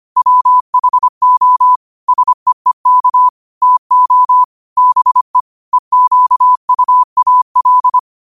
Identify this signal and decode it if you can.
Morse code, heard as a keyed tone, about 25 wpm, 1000 Hz. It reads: WHO SEEK TO BE EQUAL
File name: qotdfiles / 25wpm